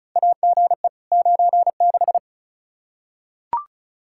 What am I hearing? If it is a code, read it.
Morse code, heard as a keyed tone, about 36 wpm, 700 Hz. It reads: AGE 96
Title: Practice QSO - Exchange 3 - More Info - 35wpm